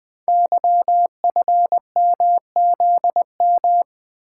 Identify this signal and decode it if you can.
Morse code, heard as a keyed tone, about 20 wpm, 700 Hz. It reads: YFMZM
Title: Code Groups - 5 Letters - 20wpm